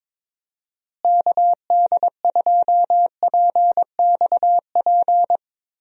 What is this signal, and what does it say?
Morse code, heard as a keyed tone, about 22 wpm, 700 Hz. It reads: KD2PXP
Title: USA Call Signs - 6 Characters - Repeated 3x - 22wpm